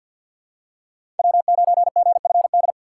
Morse code, heard as a keyed tone, about 50 wpm, 700 Hz. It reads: W9GFD
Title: USA Call Signs - 6x Character Spacing - 50wpm